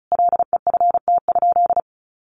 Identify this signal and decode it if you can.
Morse code, heard as a keyed tone, about 34 wpm, 700 Hz. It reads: LEFT?
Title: Sentences from Top 1200 Words 35wpm